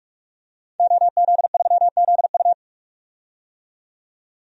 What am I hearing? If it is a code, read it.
Morse code, heard as a keyed tone, about 43 wpm, 700 Hz. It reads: OZ3ZU
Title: DXCC-100 Call Signs - 5 Characters - 45wpm